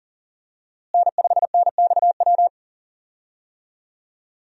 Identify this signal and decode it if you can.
Morse code, heard as a keyed tone, about 40 wpm, 700 Hz. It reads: N5NXW
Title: USA Call Signs - 5 Characters - Repeated 2x - 40wpm